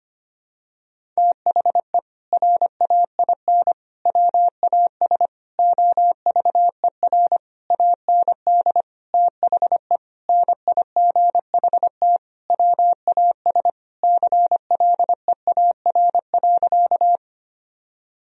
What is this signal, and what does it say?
Morse code, heard as a keyed tone, about 25 wpm, 700 Hz. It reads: THE RAIN WAS OVER AND THE NIGHT WAS CLEAR.
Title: Sentences from Top 1000 Words - Review Part 1 - 25wpm